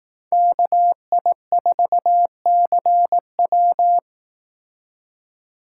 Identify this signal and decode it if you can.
Morse code, heard as a keyed tone, about 18 wpm, 700 Hz. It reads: KI4CW
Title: Speed Racing - USA Call Signs 15wpm